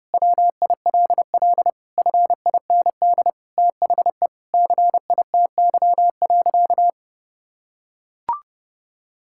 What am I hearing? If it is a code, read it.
Morse code, heard as a keyed tone, about 32 wpm, 700 Hz. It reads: WILL FIND THE CITY.
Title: Sentences from Top 1000 Words - Review Part 6 - 30wpm